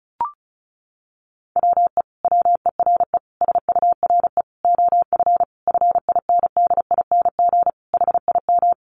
Morse code, heard as a keyed tone, about 34 wpm, 700 Hz. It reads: WE WERE SURE OF FINDING HIM
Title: Sentences from Top 1100 Words 35wpm